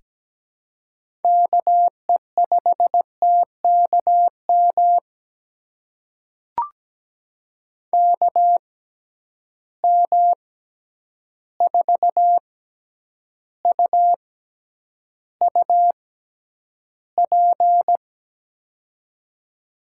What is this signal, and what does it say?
Morse code, heard as a keyed tone, about 17 wpm, spaced as for 9 wpm, 700 Hz. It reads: KE5TKM KM4UUP
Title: USA Call Signs - 6 Characters - 6x Character Spacing - 17wpm